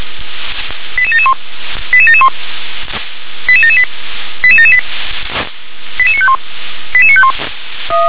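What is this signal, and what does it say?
Five-tone selective call (ZVEI), heard as quick runs of five tones, tones of about 70 ms. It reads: 88771 88771 88778 88778 88951 88951
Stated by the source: ZVEI five-tone call received on SDR.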